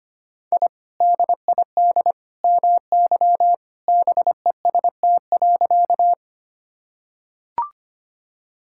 Morse code, heard as a keyed tone, about 25 wpm, 690 Hz. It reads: I DID MY BEST.
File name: practice / 25wpm / Sentences with 2-Word Phrases - 25wpm